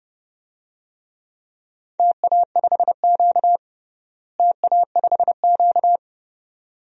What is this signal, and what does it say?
Morse code, heard as a keyed tone, about 30 wpm, 700 Hz. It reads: TA5Q TA5Q
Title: DXCC-100 Call Signs - 4 Characters - Repeated 2x - 30wpm